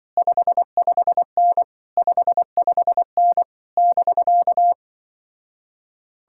Morse code, heard as a keyed tone, about 24 wpm, 700 Hz. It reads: 55N 55N <BK>
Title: Speed-Racing Common QSO Phrases Encore - 20wpm